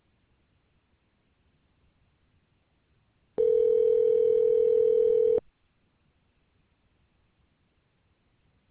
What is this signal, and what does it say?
Ringback tone (North America), 440 + 480 Hz.